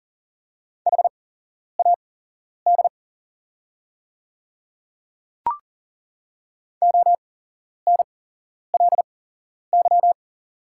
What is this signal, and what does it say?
Morse code, heard as a keyed tone, about 40 wpm, spaced as for 11 wpm, 700 Hz. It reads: HAD ONLY
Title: 8x Character Spacing - Top 200 Words - 40wpm